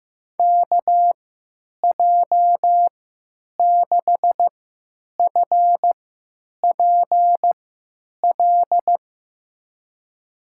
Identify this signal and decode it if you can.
Morse code, heard as a keyed tone, about 15 wpm, spaced as for 8 wpm, 700 Hz. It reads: KJ6FPL
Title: USA Call Signs - 6 Characters - 3x Character Spacing - 15wpm